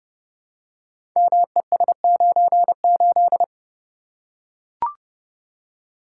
Morse code, heard as a keyed tone, about 28 wpm, 700 Hz. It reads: MES98
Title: Code Groups - 5 Letter:Numbers - 30wpm